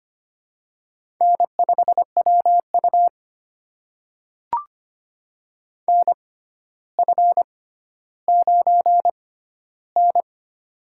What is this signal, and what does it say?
Morse code, heard as a keyed tone, about 25 wpm, spaced as for 13 wpm, 700 Hz. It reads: N5WU NF9N
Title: USA Call Signs - 4 Characters - 6x Character Spacing - 25wpm